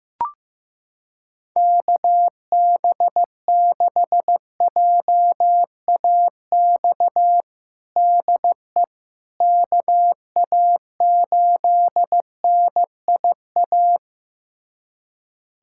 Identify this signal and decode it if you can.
Morse code, heard as a keyed tone, about 15 wpm, 700 Hz. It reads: KB6JAX DE KA8NIA
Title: USA Callsign DE Callsign - 6 Characters - 15wpm